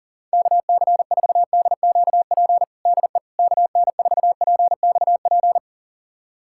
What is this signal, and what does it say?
Morse code, heard as a keyed tone, about 40 wpm, 700 Hz. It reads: KC4DQP DE KN4PXP
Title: USA Callsign DE Callsign 40wpm